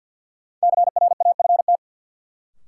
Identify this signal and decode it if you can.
Morse code, heard as a keyed tone, about 51 wpm, 700 Hz. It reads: CRAFT
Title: Sentences from Top 1000 Words - Review Part 6 - New Word Drill - 50wpm